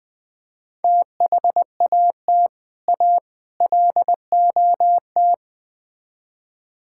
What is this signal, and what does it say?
Morse code, heard as a keyed tone, about 20 wpm, 700 Hz. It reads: THAT A LOT
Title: Build A Phrase - 5-Word Phrases - 20wpm